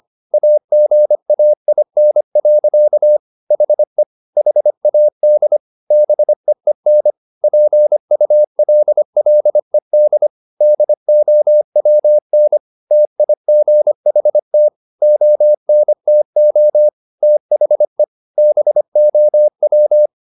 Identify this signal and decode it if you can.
Morse code, heard as a keyed tone, about 25 wpm, 600 Hz. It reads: AGAIN. HE HAD BEEN PULLED DOWN TIGHT ONTO THE BOW